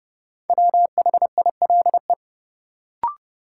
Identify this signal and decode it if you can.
Morse code, heard as a keyed tone, about 30 wpm, 700 Hz. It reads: WHILE
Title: Sentences from Top 300 Words - New Word Drill - 30wpm